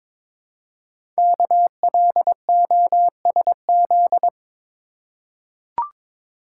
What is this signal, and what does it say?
Morse code, heard as a keyed tone, about 22 wpm, 700 Hz. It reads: KLOSZ
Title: Code Groups - 5 Letters - 22wpm